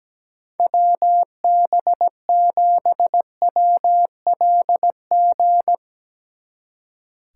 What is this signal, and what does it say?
Morse code, heard as a keyed tone, about 18 wpm, 700 Hz. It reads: WB7WLG
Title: Speed Racing - USA Call Signs 17wpm